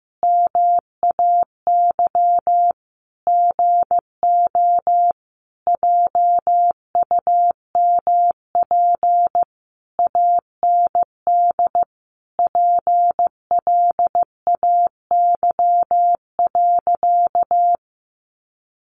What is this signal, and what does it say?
Morse code, heard as a keyed tone, about 15 wpm, 700 Hz. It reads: MAY GO JUMP AND PLAY.
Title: Sentences from Top 1300 Words 15wpm